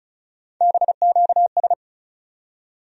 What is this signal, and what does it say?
Morse code, heard as a keyed tone, about 36 wpm, 700 Hz. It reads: BQS
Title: USA Call Sign Postfixes - 3 Letters 35wpm